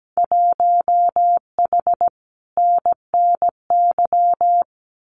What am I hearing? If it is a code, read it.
Morse code, heard as a keyed tone, about 17 wpm, 700 Hz. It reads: 1H NNY
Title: Winter Field Day Exchanges 17wpm